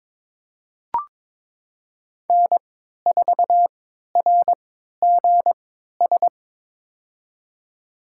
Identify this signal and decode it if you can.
Morse code, heard as a keyed tone, about 22 wpm, spaced as for 12 wpm, 700 Hz. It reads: N4RGS
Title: USA Call Signs - 3x Character Spacing - 22wpm